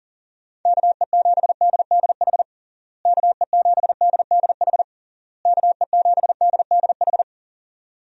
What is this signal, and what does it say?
Morse code, heard as a keyed tone, about 37 wpm, 700 Hz. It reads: KE7DDH KE7DDH KE7DDH
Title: USA Call Signs - 6 Characters - Repeated 3x - 40wpm